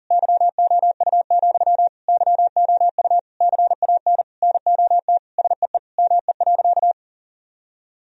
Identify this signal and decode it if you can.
Morse code, heard as a keyed tone, about 40 wpm, 700 Hz. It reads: YOU, YOU CAN NOT SEE ME.
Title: Sentences from Top 1000 Words - Review Part 3 - 40wpm